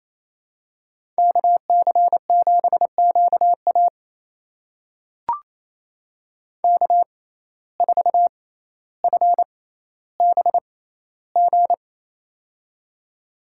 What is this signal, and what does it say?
Morse code, heard as a keyed tone, about 28 wpm, spaced as for 14 wpm, 700 Hz. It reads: KC7QA K4FBG
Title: USA Call Signs - 5 Characters - 6x Character Spacing - 28wpm